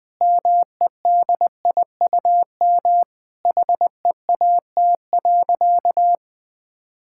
Morse code, heard as a keyed tone, about 20 wpm, 700 Hz. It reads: MEDIUM HEAT.
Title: Sentences from Top 1000 Words - Review Part 6 - 20wpm